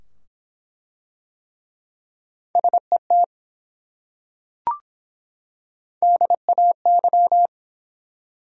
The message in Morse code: SET DAY

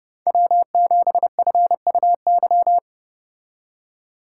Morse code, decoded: W7FUY

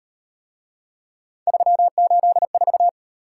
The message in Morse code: 384

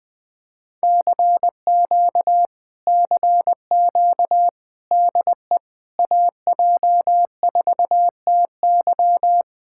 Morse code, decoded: CQ CQ DE AJ4TY

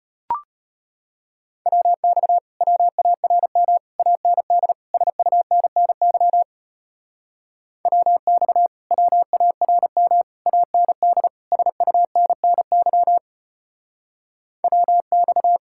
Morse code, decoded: WX WARM AND SUNNY WX WARM AND SUNNY WX

